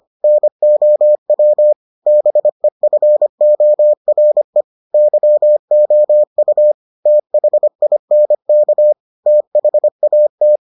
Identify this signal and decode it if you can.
Morse code, heard as a keyed tone, about 25 wpm, 600 Hz. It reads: NOW BEFORE YOU THINK THAT